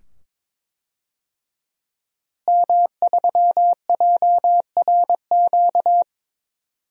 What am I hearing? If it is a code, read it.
Morse code, heard as a keyed tone, about 22 wpm, 700 Hz. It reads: M3JRQ
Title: DXCC-100 Call Signs - 5 Characters - Repeated 2x - 22wpm